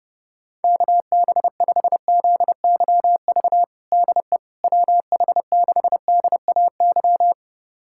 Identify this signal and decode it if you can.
Morse code, heard as a keyed tone, about 30 wpm, 700 Hz. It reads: KB5ZYV DE WH6DAY
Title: USA Callsign DE Callsign - 6 Characters - 30wpm